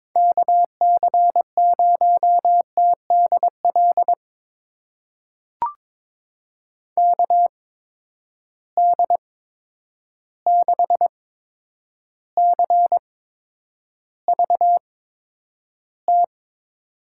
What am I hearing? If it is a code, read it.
Morse code, heard as a keyed tone, about 22 wpm, spaced as for 9 wpm, 700 Hz. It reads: KC0TDL KD6CVT
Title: USA Call Signs - 8x Character Spacing - 22wpm